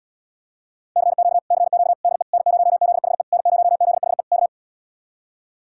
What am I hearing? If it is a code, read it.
Morse code, heard as a keyed tone, about 77 wpm, 700 Hz. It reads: CQ CQ DE N0ZHE N0ZHE K